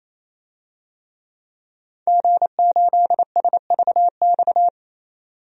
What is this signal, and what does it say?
Morse code, heard as a keyed tone, about 28 wpm, 700 Hz. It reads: G8SVX
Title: DXCC-100 Call Signs - 5 Characters - 28wpm